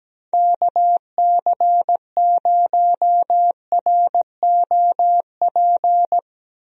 Morse code, heard as a keyed tone, about 17 wpm, 700 Hz. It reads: KC0ROP
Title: USA Call Signs - 6 Characters - Repeated 2x - 17wpm